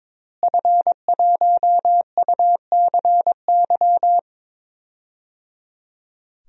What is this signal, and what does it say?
Morse code, heard as a keyed tone, about 22 wpm, 700 Hz. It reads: F1UCY
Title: DXCC-100 Call Signs - 5 Characters - Repeated 3x - 22wpm